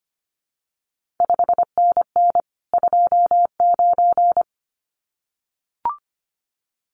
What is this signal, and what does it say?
Morse code, heard as a keyed tone, about 25 wpm, 700 Hz. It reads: 5NN 29
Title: CQ WW DX Contest Exchange Encore - 25wpm